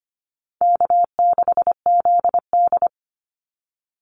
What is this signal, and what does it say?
Morse code, heard as a keyed tone, about 25 wpm, 700 Hz. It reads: K6ZD
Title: Easy Call Signs - 25wpm